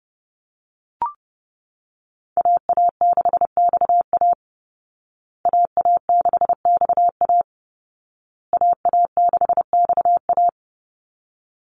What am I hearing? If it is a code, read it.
Morse code, heard as a keyed tone, about 30 wpm, 700 Hz. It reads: AA6XA AA6XA AA6XA
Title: Call Signs Repeated 2x - 30wpm